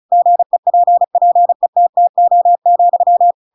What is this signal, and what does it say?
Morse code, heard as a keyed tone, about 34 wpm, 700 Hz. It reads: GEPPETTO,
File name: Pinocchio / Pinocchio ch8 35 35 ik1hge